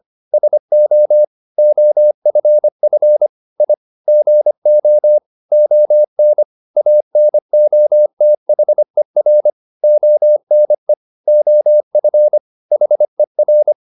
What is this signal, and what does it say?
Morse code, heard as a keyed tone, about 25 wpm, 600 Hz. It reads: SO OFF I GO ON ANOTHER ONE OF HER